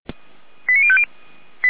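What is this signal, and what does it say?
Five-tone selective call (ZVEI), heard as quick runs of five tones, tones of about 70 ms. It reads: 89050
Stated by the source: ZVEI five-tone selective call received by SDR.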